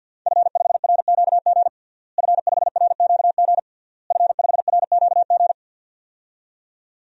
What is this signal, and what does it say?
Morse code, heard as a keyed tone, about 51 wpm, 700 Hz. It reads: F5RQG F5RQG F5RQG
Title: DXCC-100 Call Signs - 5 Characters - Repeated 3x - 50wpm